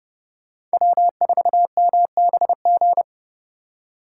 Morse code, heard as a keyed tone, about 29 wpm, 700 Hz. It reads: W4MBG